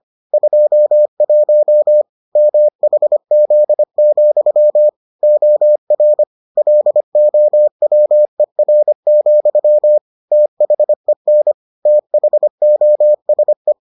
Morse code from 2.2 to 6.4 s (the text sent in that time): MHZ, OR